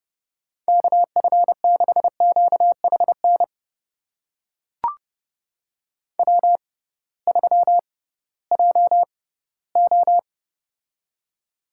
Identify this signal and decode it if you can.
Morse code, heard as a keyed tone, about 30 wpm, spaced as for 18 wpm, 700 Hz. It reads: KF6QHN W3JO
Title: USA Call Signs - 6x Character Spacing - 30wpm